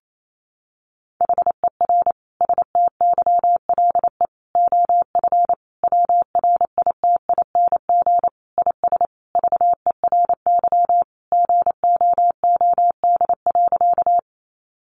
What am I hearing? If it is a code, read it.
Morse code, heard as a keyed tone, about 28 wpm, 700 Hz. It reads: HER STYLE OF WRITING IS VERY GOOD.
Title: Sentences from Top 800 Words 28wpm